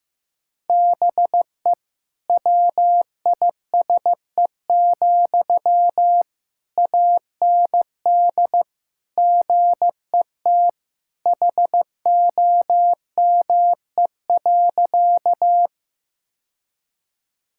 Morse code, heard as a keyed tone, about 15 wpm, 700 Hz. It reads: BE WISE, AND GET HOME.
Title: Sentences from Top 1600 Words 15wpm